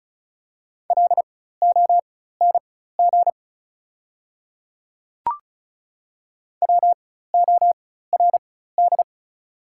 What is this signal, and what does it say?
Morse code, heard as a keyed tone, about 36 wpm, spaced as for 16 wpm, 700 Hz. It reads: LONG WORD
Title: 4x Character Spacing - Top 200 Words - 35wpm